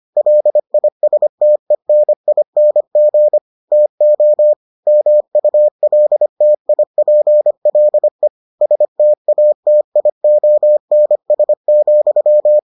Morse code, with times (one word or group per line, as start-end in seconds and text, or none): 0.2-3.4 LISTENING
3.7-4.5 TO
4.9-8.3 MULTIPLE
8.6-12.6 STATIONS,